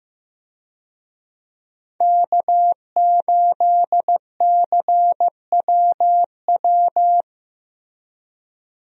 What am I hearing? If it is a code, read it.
Morse code, heard as a keyed tone, about 15 wpm, 700 Hz. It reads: K8CWW
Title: USA Call Signs - 5 Characters - Repeated 2x - 15wpm